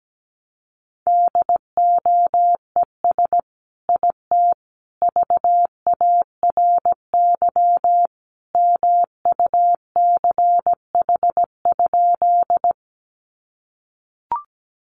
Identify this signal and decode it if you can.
Morse code, heard as a keyed tone, about 17 wpm, 700 Hz. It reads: DOES IT VARY MUCH?
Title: Sentences from Top 1200 Words 17wpm